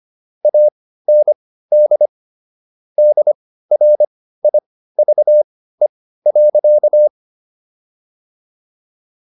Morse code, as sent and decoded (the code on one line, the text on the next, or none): .- -. -.. / -.. .-. .. ...- . .-.-.-
AND DRIVE.